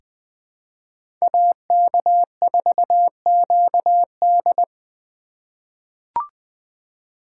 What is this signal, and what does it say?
Morse code, heard as a keyed tone, about 20 wpm, 700 Hz. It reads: AK4QD
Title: USA Call Signs - 5 Characters - 3x Character Spacing - 20wpm